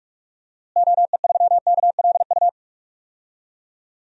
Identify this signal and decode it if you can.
Morse code, heard as a keyed tone, about 46 wpm, 700 Hz. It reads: OE3KLU
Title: DXCC-100 Call Signs - 6 Characters - Repeated 3x - 45wpm